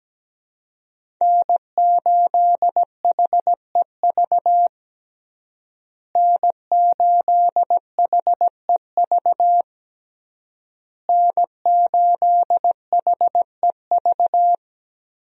Morse code, heard as a keyed tone, about 17 wpm, 700 Hz. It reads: N8HEV N8HEV N8HEV
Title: USA Call Signs - 5 Characters - Repeated 3x - 17wpm